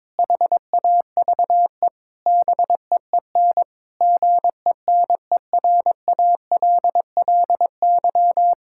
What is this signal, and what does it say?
Morse code, heard as a keyed tone, about 22 wpm, 700 Hz. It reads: HAVE BEEN GENERALLY
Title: Sentences from Top 1000 Words - Review Part 6 - 22wpm